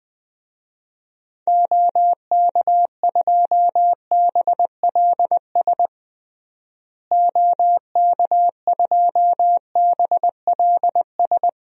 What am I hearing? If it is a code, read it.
Morse code, heard as a keyed tone, about 19 wpm, 700 Hz. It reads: OK2BLS OK2BLS
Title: DXCC-100 Call Signs - 6 Characters - Repeated 2x - 20wpm